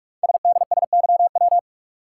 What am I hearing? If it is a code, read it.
Morse code, heard as a keyed tone, about 44 wpm, 700 Hz. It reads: SDSYW
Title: Code Groups - 5 Letter:Numbers - 45wpm